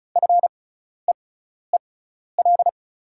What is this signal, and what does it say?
Morse code, heard as a keyed tone, about 36 wpm, spaced as for 12 wpm, 690 Hz. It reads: FEEL
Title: 6x Character Spacing - Top 500 Words - 35wpm